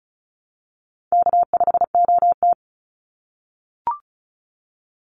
Morse code, heard as a keyed tone, about 34 wpm, 700 Hz. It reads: K5OT
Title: NA SOTA Chaser Callsigns 35wpm